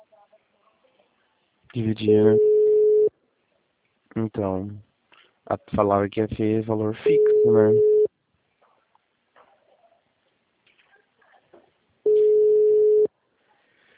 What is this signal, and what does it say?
Ringback tone (Europe, 425 Hz), with speech.
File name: out- -2010- - - .0